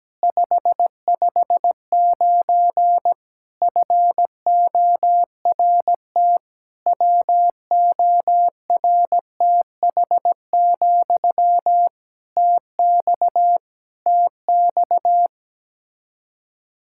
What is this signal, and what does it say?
Morse code, heard as a keyed tone, about 17 wpm, 700 Hz. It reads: 559 FORT WORTH, TX TX
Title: Practice QSO - Exchange 1 - Beginning - 17wpm